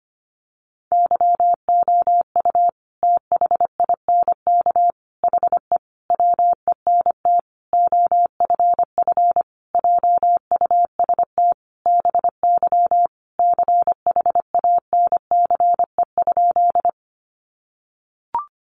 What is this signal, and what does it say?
Morse code, heard as a keyed tone, about 25 wpm, 700 Hz. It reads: YOU THINK HE WENT OFF JUST BY CHANCE?
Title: Sentences from Top 800 Words 25wpm